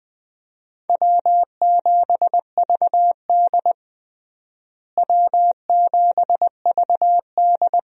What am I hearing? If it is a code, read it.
Morse code, heard as a keyed tone, about 20 wpm, 700 Hz. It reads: W7VD W7VD
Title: USA Call Signs - 4 Characters - Repeated 2x - 20wpm